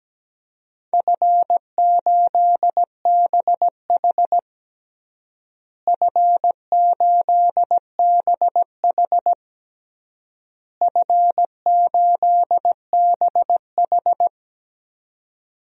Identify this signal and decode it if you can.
Morse code, heard as a keyed tone, about 17 wpm, 700 Hz. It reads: F8BH F8BH F8BH
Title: DXCC-100 Call Signs - 4 Characters - Repeated 3x - 17wpm